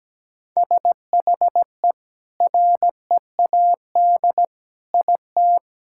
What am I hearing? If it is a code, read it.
Morse code, heard as a keyed tone, about 17 wpm, 700 Hz. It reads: SHE READ IT
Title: Sentences from Top 1000 Words - Review Part 6 - 17wpm